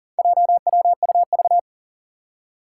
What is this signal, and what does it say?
Morse code, heard as a keyed tone, about 37 wpm, 700 Hz. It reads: JWUV
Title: Four Challenging Letter-Number Combinations 40wpm